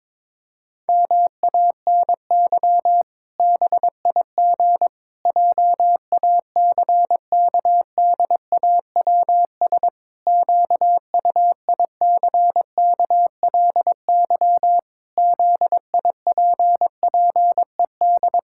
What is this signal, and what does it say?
Morse code, heard as a keyed tone, about 22 wpm, 700 Hz. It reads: MANY BIG JACKDAWS QUICKLY ZIPPED
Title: Pangram Challenge v2 - 22wpm